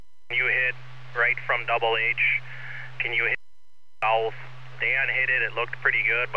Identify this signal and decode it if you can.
Emergency Alert System (SAME) tones, heard as harsh buzzing bursts, with speech. It is heard from 1.1 to 5.2 s.